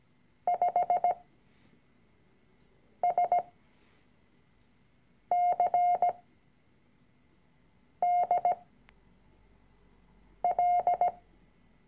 Morse code, heard as a keyed tone, about 17 wpm, spaced as for 4 wpm, 710 Hz. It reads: 5SCDL
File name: CW-Online-Kurs / Anfänger / Lektion 006